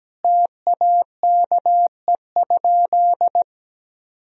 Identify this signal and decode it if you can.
Morse code, heard as a keyed tone, about 17 wpm, 700 Hz. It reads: TAKE?
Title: Sentences from Top 1000 Words - Review Part 2 - 17wpm